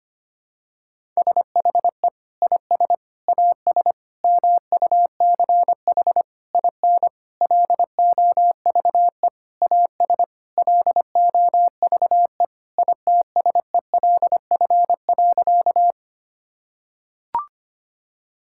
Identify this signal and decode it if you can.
Morse code, heard as a keyed tone, about 25 wpm, 700 Hz. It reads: SHE IS AS MUCH IN LOVE AS LOVE ITSELF.
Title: Sentences from Top 1000 Words - Review Part 5 - 25wpm